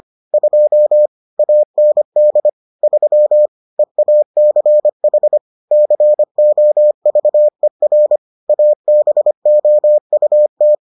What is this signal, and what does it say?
Morse code, heard as a keyed tone, about 25 wpm, 600 Hz. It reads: 2 AND 3 EACH COVER ABOUT